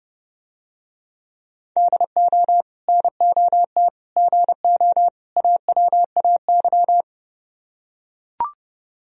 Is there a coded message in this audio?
Yes: Morse code from 1.8 to 7.0 s, heard as a keyed tone, about 30 wpm, 700 Hz. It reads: DO NOT GO AWAY